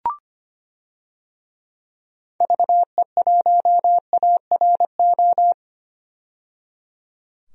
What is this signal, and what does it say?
Morse code, heard as a keyed tone, about 25 wpm, 700 Hz. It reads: VE1ARO
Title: DXCC-100 Call Signs - 6 Characters - 25wpm